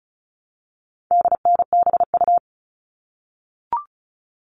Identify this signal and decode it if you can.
Morse code, heard as a keyed tone, about 34 wpm, 700 Hz. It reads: DNBU
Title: All Cut Numbers - 4-Digit 35wpm